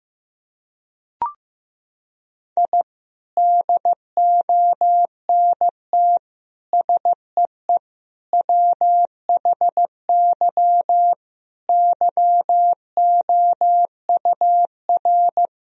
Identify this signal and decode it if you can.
Morse code, heard as a keyed tone, about 16 wpm, 700 Hz. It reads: I DONT SEE WHY YOUR
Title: Sentences from Top 1000 Words - Review Part 2 - 15wpm